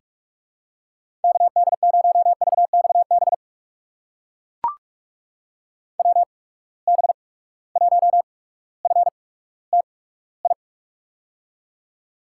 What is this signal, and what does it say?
Morse code, heard as a keyed tone, about 44 wpm, spaced as for 19 wpm, 700 Hz. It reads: KD0VXB WB1FTI
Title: USA Call Signs - 6 Characters - 8x Character Spacing - 45wpm